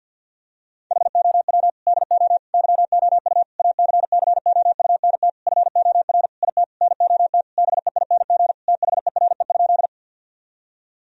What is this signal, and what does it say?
Morse code, heard as a keyed tone, about 50 wpm, 700 Hz. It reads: HOW DO YOU ACCOUNT FOR IT NOT BEING THERE?
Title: Sentences from Top 1000 Words - Review Part 3 - 50wpm